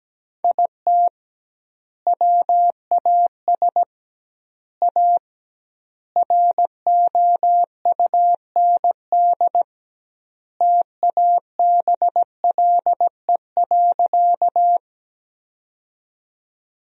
Morse code, heard as a keyed tone, about 17 wpm, 700 Hz. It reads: IT WAS A ROUND TABLE.
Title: 2x Word-Spacing - Sentences from Top 400 Words - 17wpm